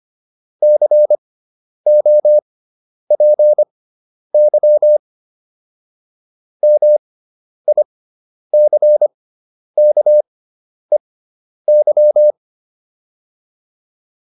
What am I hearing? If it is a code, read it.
Morse code, heard as a keyed tone, about 25 wpm, spaced as for 10 wpm, 600 Hz. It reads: COPY MICKEY
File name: QSO310 10